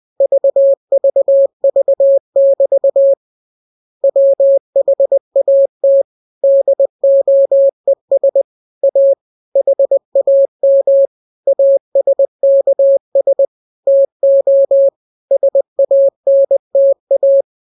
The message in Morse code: VVV<BT> WHAT DOES A HAM ASKS TO SANTA